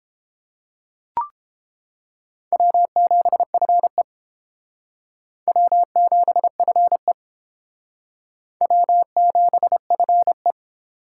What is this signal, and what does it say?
Morse code, heard as a keyed tone, about 32 wpm, 700 Hz. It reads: W7FE W7FE W7FE